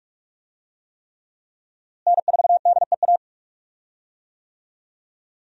Morse code, heard as a keyed tone, about 41 wpm, 700 Hz. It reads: N4DEA
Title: USA Call Signs - 5 Characters - Repeated 3x - 45wpm